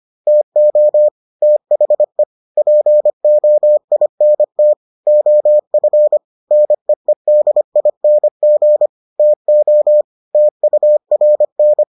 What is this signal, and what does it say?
Morse code, heard as a keyed tone, about 25 wpm, 600 Hz. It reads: TO THE POINT OF NEEDING TO TURN